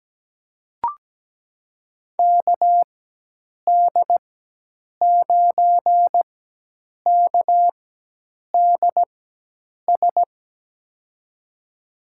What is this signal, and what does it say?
Morse code, heard as a keyed tone, about 17 wpm, spaced as for 8 wpm, 700 Hz. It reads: KD9KDS
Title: USA Call Signs - 6 Characters - 4x Character Spacing - 17wpm